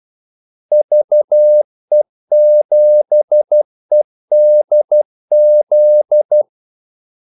Morse code, heard as a keyed tone, about 12 wpm, 600 Hz. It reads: VE7EDZ
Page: 2 ekstra ordmellemrum mellem kaldesignalerne